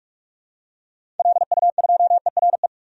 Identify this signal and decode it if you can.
Morse code, heard as a keyed tone, about 46 wpm, 700 Hz. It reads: LU2ERE
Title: DXCC-100 Call Signs - 6 Characters - 45wpm